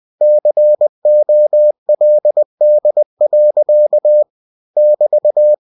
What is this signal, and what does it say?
Morse code, heard as a keyed tone, about 20 wpm, 600 Hz. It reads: COLD. <BT>
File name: cw-books / winnie-the-pooh / winnie-20wpm-page-0087